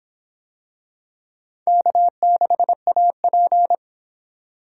Morse code, heard as a keyed tone, about 25 wpm, 700 Hz. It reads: K6AP